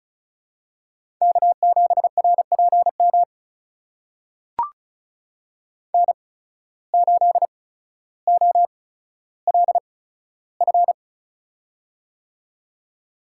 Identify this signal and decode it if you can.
Morse code, heard as a keyed tone, about 34 wpm, spaced as for 15 wpm, 700 Hz. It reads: K7RPM N8OLF
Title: USA Call Signs - 5 Characters - 8x Character Spacing - 35wpm